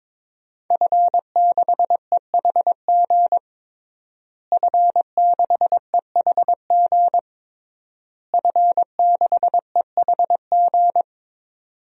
Morse code, heard as a keyed tone, about 22 wpm, 700 Hz. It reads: F6EHG F6EHG F6EHG
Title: DXCC-100 Call Signs - 5 Characters - Repeated 3x - 22wpm